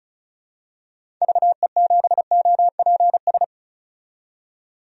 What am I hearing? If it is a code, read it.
Morse code, heard as a keyed tone, about 36 wpm, 700 Hz. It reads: VE7OPS
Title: DXCC-100 Call Signs - 6 Characters - 35wpm